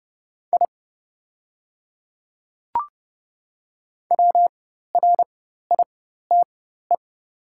Morse code, heard as a keyed tone, about 30 wpm, spaced as for 14 wpm, 700 Hz. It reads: I WRITE
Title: 4x Character Spacing - Top 100 Words - 30wpm